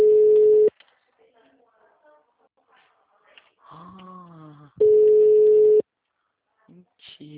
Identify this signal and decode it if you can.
Ringback tone (Europe, 425 Hz), with speech.